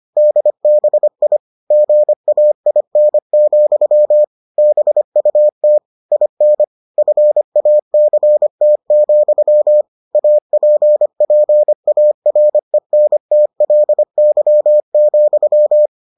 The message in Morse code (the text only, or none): DBI GAIN, BUT IN FACT, APPARENTLY,